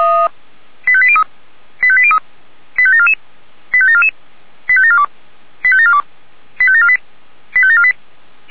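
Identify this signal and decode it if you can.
Five-tone selective call (ZVEI), heard as quick runs of five tones, tones of about 70 ms. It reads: 85803 85803 85744 85744 85742 85742 85748 85748
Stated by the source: ZVEI five-tone call 85719 received by SDR.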